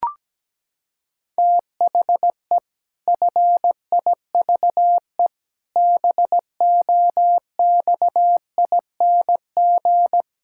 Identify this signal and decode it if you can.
Morse code, heard as a keyed tone, about 17 wpm, 700 Hz. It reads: THE FIVE BOXING
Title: Pangram Challenge v2 - 17wpm